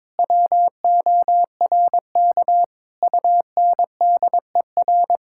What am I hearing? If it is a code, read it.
Morse code, heard as a keyed tone, about 21 wpm, 700 Hz. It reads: WORK UNDER
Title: Sentences from Top 1000 Words - Review Part 6 - 22wpm